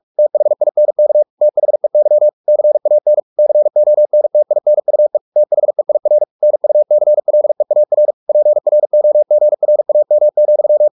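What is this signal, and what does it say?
Morse code, heard as a keyed tone, about 44 wpm, 600 Hz. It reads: THINK THEY CAN CONTINUE THEIR NUCLEAR PROGRAM,
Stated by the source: Morse practice at 45 wpm, the novel Never by Ken Follett in English, from CW med Gnister.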